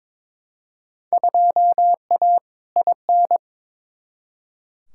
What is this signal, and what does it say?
Morse code, heard as a keyed tone, about 22 wpm, 700 Hz. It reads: 2A IN
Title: Field Day Exchanges - 22wpm